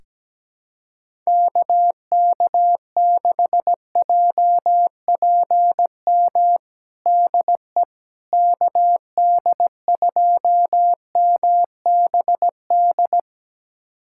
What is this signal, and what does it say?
Morse code, heard as a keyed tone, about 17 wpm, 700 Hz. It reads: KK6JPM DE KD2MBD
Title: USA Callsign DE Callsign - 6 Characters - 17wpm